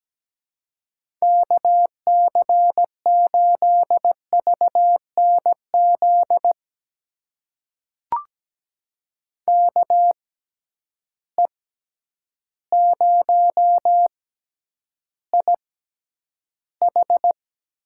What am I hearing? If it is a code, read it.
Morse code, heard as a keyed tone, about 17 wpm, spaced as for 9 wpm, 700 Hz. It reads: KC8VNZ KE0IH
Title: USA Call Signs - 6 Characters - 6x Character Spacing - 17wpm